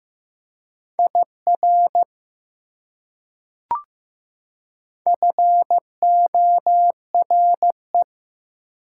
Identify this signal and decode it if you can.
Morse code, heard as a keyed tone, about 15 wpm, 700 Hz. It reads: IR FORE
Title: Prefixes and Suffixes 15wpm